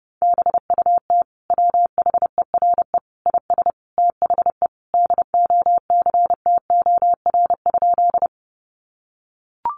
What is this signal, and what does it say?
Morse code, heard as a keyed tone, about 30 wpm, 700 Hz. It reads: BUT WHERE IS THE DOCTOR?
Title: Sentences from Top 1500 Words 30wpm